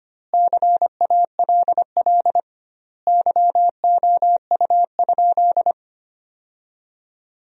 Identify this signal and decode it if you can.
Morse code, heard as a keyed tone, about 25 wpm, 700 Hz. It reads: CALL YOU?
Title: 2x Word-Spacing - Sentences from Top 100 Words - 25wpm